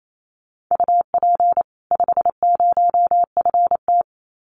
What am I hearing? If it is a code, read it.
Morse code, heard as a keyed tone, about 28 wpm, 700 Hz. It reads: UP 50FT
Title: Common QSO Phrases 28wpm